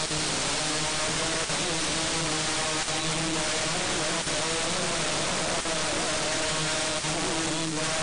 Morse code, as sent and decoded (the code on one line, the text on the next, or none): none